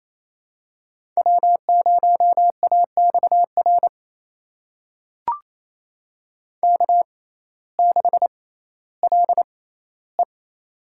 Morse code, heard as a keyed tone, about 28 wpm, spaced as for 15 wpm, 700 Hz. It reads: W0AXR K6LE
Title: USA Call Signs - 6x Character Spacing - 28wpm